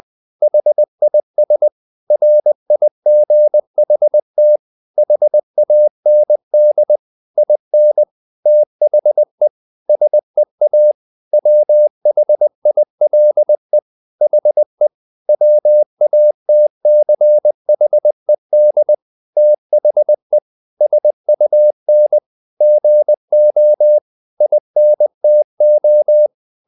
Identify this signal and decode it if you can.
Morse code, heard as a keyed tone, about 20 wpm, 600 Hz. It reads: HIS RIGHT HAND IN THE SEA WHILE HE WATCHED THE SUN GO INTO
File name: the old man and the sea part 2 20wpm